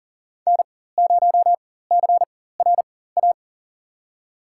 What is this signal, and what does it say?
Morse code, heard as a keyed tone, about 40 wpm, spaced as for 19 wpm, 700 Hz. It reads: N0CRA
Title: USA Call Signs - 5 Characters - 4x Character Spacing - 40wpm